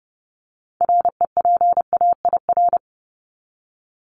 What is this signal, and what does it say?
Morse code, heard as a keyed tone, about 30 wpm, 700 Hz. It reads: REPAIR
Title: Top 1501-1750 Words 30wpm